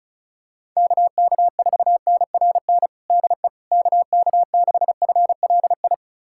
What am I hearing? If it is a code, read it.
Morse code, heard as a keyed tone, about 36 wpm, 700 Hz. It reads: KK4NRN DE KK6FLI
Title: USA Callsign DE Callsign - 6 Characters - 35wpm